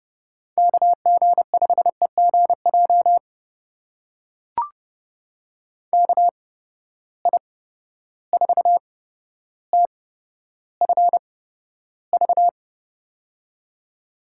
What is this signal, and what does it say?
Morse code, heard as a keyed tone, about 29 wpm, spaced as for 13 wpm, 700 Hz. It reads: KG5EGJ KI4TFV
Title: USA Call Signs - 8x Character Spacing - 30wpm